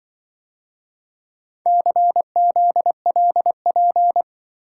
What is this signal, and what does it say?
Morse code, heard as a keyed tone, about 24 wpm, 700 Hz. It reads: CZLP
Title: Speed-Racing - Four Letter-Number Combinations - 20wpm